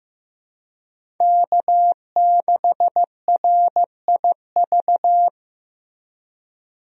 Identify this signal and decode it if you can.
Morse code, heard as a keyed tone, about 15 wpm, 700 Hz. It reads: K6RIV
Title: USA Call Signs - 5 Characters - Repeated 3x - 15wpm